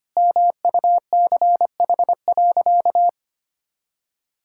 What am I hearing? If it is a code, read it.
Morse code, heard as a keyed tone, about 25 wpm, 700 Hz. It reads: MUCH.
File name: practice / 25wpm / Sentences with 3-Word Phrases - 25wpm